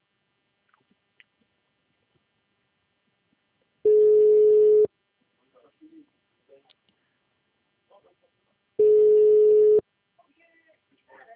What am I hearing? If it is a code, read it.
Ringback tone (Europe, 425 Hz), with speech.
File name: out- -2018- - - .23661